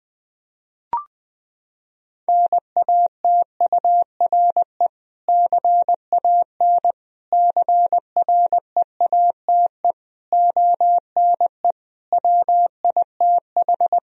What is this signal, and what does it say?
Morse code, heard as a keyed tone, about 21 wpm, 700 Hz. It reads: NATURE CAN CREATE ONE WITH
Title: Sentences from Top 1000 Words - Review Part 3 - 20wpm